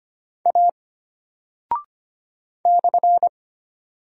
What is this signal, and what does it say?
Morse code, heard as a keyed tone, about 25 wpm, 700 Hz. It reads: A /
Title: Single Character (Letter:Number:Punctuation:Prosign) Alt 25wpm